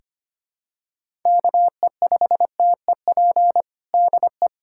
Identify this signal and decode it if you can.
Morse code, heard as a keyed tone, about 25 wpm, 700 Hz. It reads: KE5TEP DE
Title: USA Callsign DE Callsign - 6 Characters - 25wpm